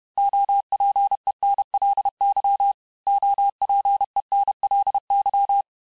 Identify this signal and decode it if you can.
Morse code, heard as a keyed tone, about 31 wpm, 800 Hz. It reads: OPENLY OPENLY
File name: cours-du-mercredi / audios / cours-861-texte-les-mots-vitesse-29MPM